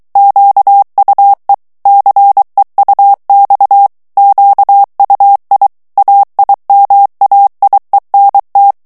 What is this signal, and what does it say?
Morse code, heard as a keyed tone, about 23 wpm, 800 Hz. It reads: QUE CEUX QUI AIMAIENT
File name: FRANCAIS TEXTE CLAIR 2 VITESSE 22 MOTS MINUTES